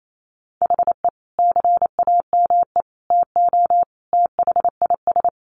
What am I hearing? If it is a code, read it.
Morse code, heard as a keyed tone, about 28 wpm, 700 Hz. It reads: HE CAME TO THIS
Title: A Sound Of Thunder 28wpm